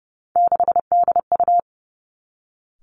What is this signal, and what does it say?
Morse code, heard as a keyed tone, about 30 wpm, 700 Hz. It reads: 6DU
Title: Three Challenging Letter-Number Combinations 30wpm